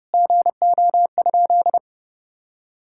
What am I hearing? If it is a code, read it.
Morse code, heard as a keyed tone, about 28 wpm, 700 Hz. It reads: GO?
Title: Sentences from Top 1600 Words 30wpm